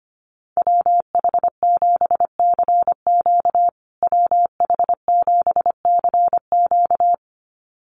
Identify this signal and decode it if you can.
Morse code, heard as a keyed tone, about 25 wpm, 700 Hz. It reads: WH7CQ WH7CQ
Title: USA Call Signs Repeated 2x 25wpm